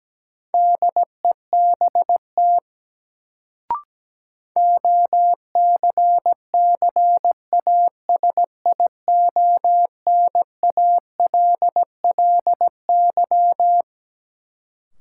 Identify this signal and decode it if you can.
Morse code, heard as a keyed tone, about 17 wpm, 700 Hz. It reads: DEBT OCCASIONALLY
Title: Sentences from Top 1300 Words - New Word Drill - 17wpm